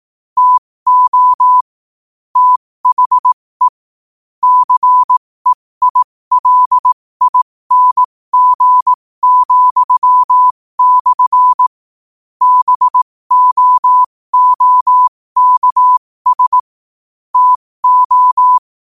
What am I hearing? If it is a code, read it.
Morse code, heard as a keyed tone, about 18 wpm, spaced as for 15 wpm, 1000 Hz. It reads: TO THE CEILING,/ BOOKS TO
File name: qotdfiles / 15wpm